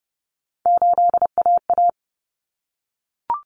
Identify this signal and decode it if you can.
Morse code, heard as a keyed tone, about 30 wpm, 700 Hz. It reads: 8AA
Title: Common Cut Numbers - 3-Digit 30wpm